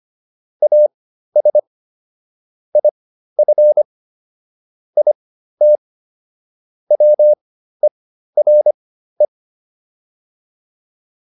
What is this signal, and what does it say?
Morse code, heard as a keyed tone, about 25 wpm, spaced as for 13 wpm, 600 Hz. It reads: AS IF IT WERE